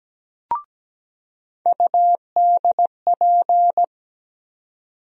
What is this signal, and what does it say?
Morse code, heard as a keyed tone, about 17 wpm, 700 Hz. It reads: UDP
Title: USA Call Sign Postfixes - 3 Letters 17wpm